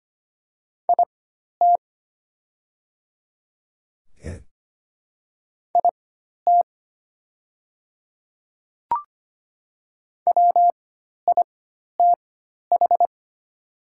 Morse code, heard as a keyed tone, about 25 wpm, spaced as for 12 wpm, 700 Hz. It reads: IT IT WITH
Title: 4x Character Spacing - Top 200 Words - 25wpm